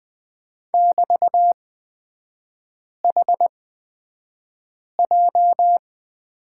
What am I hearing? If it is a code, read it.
Morse code, heard as a keyed tone, about 20 wpm, spaced as for 5 wpm, 700 Hz. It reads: <BT>HJ
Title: 224 Lesson 36e - BT - Three Characters 20wpm